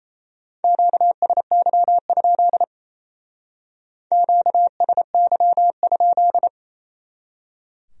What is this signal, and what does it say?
Morse code, heard as a keyed tone, about 32 wpm, 700 Hz. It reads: QSY? QSY?
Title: Speed-Racing Q-Codes - 28wpm